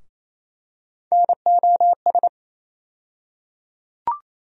Morse code, heard as a keyed tone, about 28 wpm, 700 Hz. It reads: NOS
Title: USA Call Sign Postfixes - 3 Letters 28wpm